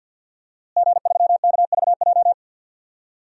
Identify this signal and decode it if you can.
Morse code, heard as a keyed tone, about 52 wpm, 700 Hz. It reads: G3KVJ
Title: DXCC-100 Call Signs - 5 Characters - Repeated 2x - 50wpm